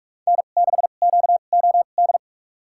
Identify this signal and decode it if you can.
Morse code, heard as a keyed tone, about 46 wpm, spaced as for 32 wpm, 700 Hz. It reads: N6QOD
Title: USA Call Signs - 2x Character Spacing - 45wpm